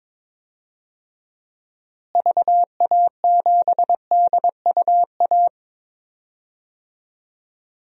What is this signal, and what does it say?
Morse code, heard as a keyed tone, about 22 wpm, 700 Hz. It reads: VA7DUA
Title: DXCC-100 Call Signs - 6 Characters - Repeated 2x - 22wpm